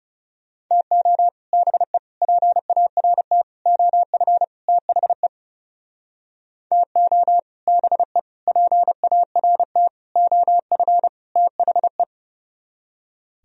Morse code, heard as a keyed tone, about 37 wpm, 700 Hz. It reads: TO BE PART OF THE TO BE PART OF THE
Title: Speed-Racing - Sets of 5 Words 30wpm